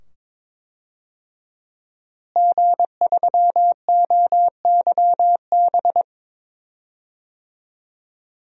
Morse code, heard as a keyed tone, about 22 wpm, 700 Hz. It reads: G3OYB
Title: DXCC-100 Call Signs - 5 Characters - Repeated 2x - 22wpm